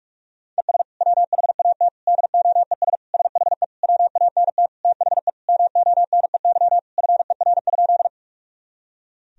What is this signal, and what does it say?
Morse code, heard as a keyed tone, about 46 wpm, 700 Hz. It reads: ES WHAT DOES SHE WANT THE MONEY FER?
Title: Sentences from Top 400 Words and Abbreviations 45wpm